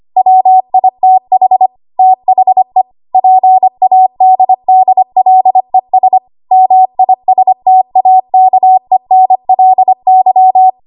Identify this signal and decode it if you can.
Morse code, heard as a keyed tone, about 25 wpm, 750 Hz. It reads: WITH THE PADDLES MISTAKENLY
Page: 25 WPM Code Archive